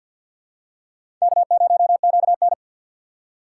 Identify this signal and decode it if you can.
Morse code, heard as a keyed tone, about 51 wpm, 700 Hz. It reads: K0QN
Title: USA Call Signs - 4 Characters - 3x Character Spacing - 50wpm